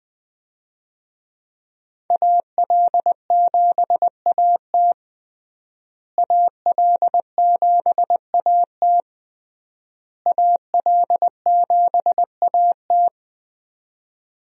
Morse code, heard as a keyed tone, about 21 wpm, 700 Hz. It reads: AL7AT AL7AT AL7AT
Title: USA Call Signs - 5 Characters - Repeated 3x - 20wpm